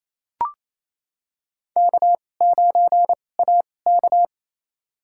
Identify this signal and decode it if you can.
Morse code, heard as a keyed tone, about 28 wpm, spaced as for 20 wpm, 700 Hz. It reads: K9AK
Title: USA Call Signs - 4 Characters - 2x Character Spacing - 28wpm